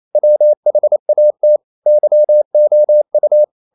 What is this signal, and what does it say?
Morse code, heard as a keyed tone, about 28 wpm, 600 Hz. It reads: WHAT YOU
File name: CW / Practice / EnglishQuotations0228